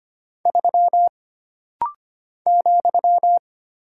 Morse code, heard as a keyed tone, about 25 wpm, 700 Hz. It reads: 3 ,
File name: Single Character (Letter:Number:Punctuation:Prosign) - Rapid-Fire - 25wpm